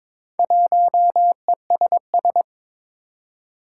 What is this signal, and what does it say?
Morse code, heard as a keyed tone, about 23 wpm, 700 Hz. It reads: 1ESS
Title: Four Challenging Letter-Number Combinations 22wpm